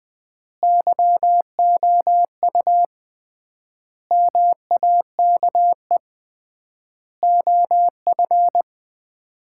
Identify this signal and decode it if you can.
Morse code, heard as a keyed tone, about 20 wpm, 700 Hz. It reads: YOU MAKE OF
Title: 3x Word-Spacing - Sets of 5 Words - 20wpm